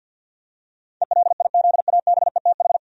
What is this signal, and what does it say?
Morse code, heard as a keyed tone, about 49 wpm, 700 Hz. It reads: ELIZABETH
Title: USA Common Names - 50wpm